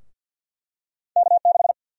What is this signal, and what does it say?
Morse code, heard as a keyed tone, about 52 wpm, 700 Hz. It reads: K6
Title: USA Call Sign Prefixes - Repeated 2x - 50wpm